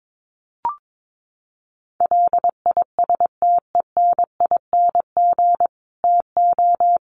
Morse code, heard as a keyed tone, about 22 wpm, 700 Hz. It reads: LISTENING TO
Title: Sets of 5 Words 22wpm